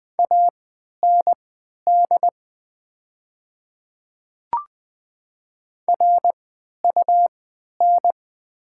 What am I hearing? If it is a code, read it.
Morse code, heard as a keyed tone, about 20 wpm, spaced as for 11 wpm, 700 Hz. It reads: AND RUN
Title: 3x Character Spacing - Top 500 Words - 20wpm